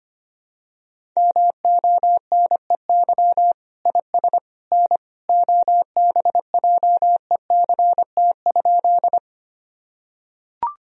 Morse code, heard as a keyed tone, about 25 wpm, 700 Hz. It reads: MONEY IS N OBJECT?
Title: Sentences from Top 500 Words and Abbreviations 25wpm